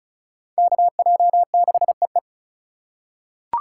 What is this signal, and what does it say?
Morse code, heard as a keyed tone, about 34 wpm, 700 Hz. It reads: KJ6EE
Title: USA Call Signs - 5 Characters - 3x Character Spacing - 35wpm